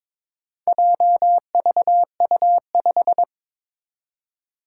Morse code, heard as a keyed tone, about 22 wpm, 700 Hz. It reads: JVU5
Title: Four Challenging Letter-Number Combinations 22wpm